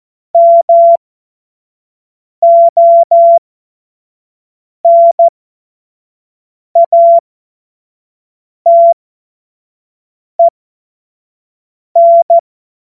Morse code, heard as a keyed tone, about 14 wpm, spaced as for 5 wpm, 680 Hz. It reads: MONATEN
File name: Morsekurs/Witze